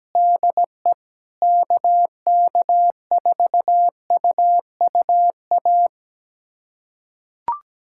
Morse code, heard as a keyed tone, about 17 wpm, 700 Hz. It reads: DE KK4UUA
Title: USA Callsign DE Callsign 17wpm